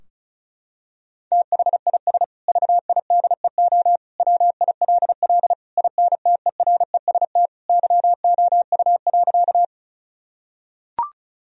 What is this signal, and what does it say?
Morse code, heard as a keyed tone, about 34 wpm, 700 Hz. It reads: THIS VIDEO WILL INTEREST YOU.
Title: Sentences from Top 1000 Words - Review Part 4 - 35wpm